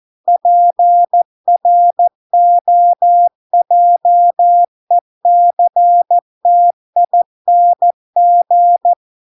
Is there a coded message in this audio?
Yes: Morse code from 0.3 to 8.9 s, heard as a keyed tone, about 14 wpm, 700 Hz. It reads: PROJECTING